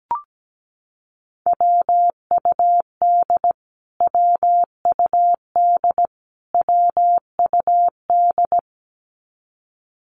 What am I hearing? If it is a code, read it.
Morse code, heard as a keyed tone, about 17 wpm, 700 Hz. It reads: WUD WUD WUD
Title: Abbreviations Repeated 3x - 17wpm